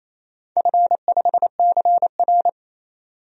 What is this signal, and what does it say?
Morse code, heard as a keyed tone, about 28 wpm, 700 Hz. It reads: F5CR
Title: DXCC-100 Call Signs - 4 Characters - 28wpm